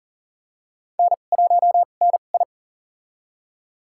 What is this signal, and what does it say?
Morse code, heard as a keyed tone, about 40 wpm, spaced as for 29 wpm, 700 Hz. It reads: N1NI
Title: USA Call Signs - 4 Characters - 2x Character Spacing - 40wpm